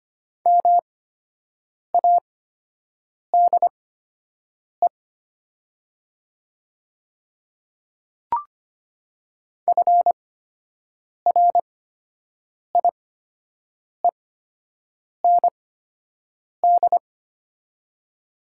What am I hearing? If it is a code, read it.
Morse code, heard as a keyed tone, about 25 wpm, spaced as for 7 wpm, 700 Hz. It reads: MADE FRIEND
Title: 8x Character Spacing - Top 300 Words - 25wpm